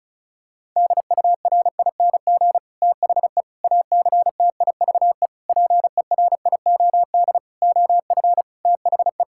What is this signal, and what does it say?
Morse code, heard as a keyed tone, about 34 wpm, 700 Hz. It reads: DURING THE ACTIVE PERIOD OF THE
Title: Sentences from Top 1000 Words - Review Part 6 - 35wpm